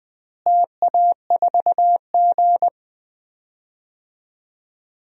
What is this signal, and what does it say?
Morse code, heard as a keyed tone, about 20 wpm, 700 Hz. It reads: TA4G
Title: DXCC-100 Call Signs - 4 Characters - Repeated 2x - 20wpm